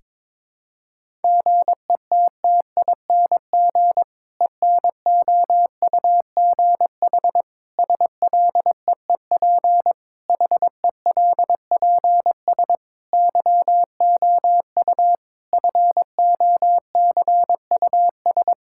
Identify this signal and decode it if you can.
Morse code, heard as a keyed tone, about 21 wpm, 700 Hz. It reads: GETTING ENOUGH SLEEP HELPS YOU FOCUS
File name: Difficult Sentences with Hints - 22wpm